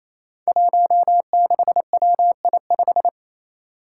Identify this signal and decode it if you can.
Morse code, heard as a keyed tone, about 27 wpm, 700 Hz. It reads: 16WI5
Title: Five Challenging Letter-Number Combinations 28wpm